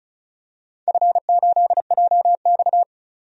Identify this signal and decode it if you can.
Morse code, heard as a keyed tone, about 36 wpm, 700 Hz. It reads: F8JX
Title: DXCC-100 Call Signs - 4 Characters - 35wpm